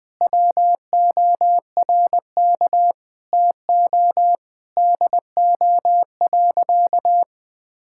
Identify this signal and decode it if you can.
Morse code, heard as a keyed tone, about 19 wpm, 700 Hz. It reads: WORK TO DO.
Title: Sentences with 5-Word Phrases - Spoken Beginning - 20wpm